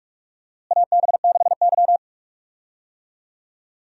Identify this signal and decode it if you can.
Morse code, heard as a keyed tone, about 46 wpm, 700 Hz. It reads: AB6Y
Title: USA Call Signs - 4 Characters - Repeated 2x - 45wpm